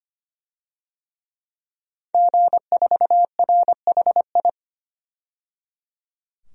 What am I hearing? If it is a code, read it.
Morse code, heard as a keyed tone, about 25 wpm, 700 Hz. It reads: G4RHI
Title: DXCC-100 Call Signs - 5 Characters - 25wpm